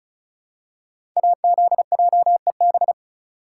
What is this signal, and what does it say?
Morse code, heard as a keyed tone, about 36 wpm, 700 Hz. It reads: AZJEB
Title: Code Groups - 5 Letter:Numbers - 35wpm